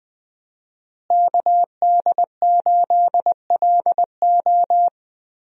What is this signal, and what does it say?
Morse code, heard as a keyed tone, about 21 wpm, 700 Hz. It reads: KD8LO
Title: USA Call Signs - 5 Characters - Repeated 3x - 20wpm